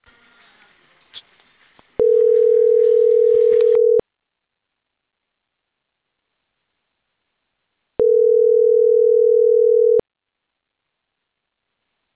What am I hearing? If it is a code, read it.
Ringback tone (North America), 440 + 480 Hz.